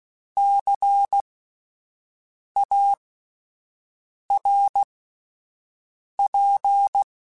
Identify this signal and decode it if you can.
Morse code, heard as a keyed tone, about 16 wpm, spaced as for 5 wpm, 800 Hz. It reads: CARP
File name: Spaced Alphas / Spaced Alphas mp3s / Cs sp @10